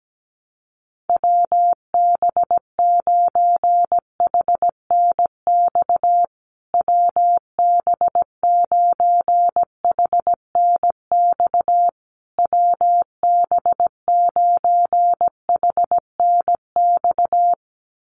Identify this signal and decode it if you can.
Morse code, heard as a keyed tone, about 17 wpm, 700 Hz. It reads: WB9HNX WB9HNX WB9HNX
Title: USA Call Signs Repeated 3x 17wpm